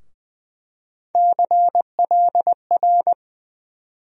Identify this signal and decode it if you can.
Morse code, heard as a keyed tone, about 21 wpm, 700 Hz. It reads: CLR
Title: USA Call Sign Postfixes - 3 Letters 20wpm